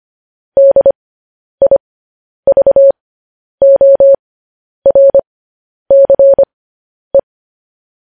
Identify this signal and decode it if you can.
Morse code, heard as a keyed tone, about 25 wpm, spaced as for 10 wpm, 550 Hz. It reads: DIVORCE
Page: news headlines in morse code at 10 wpm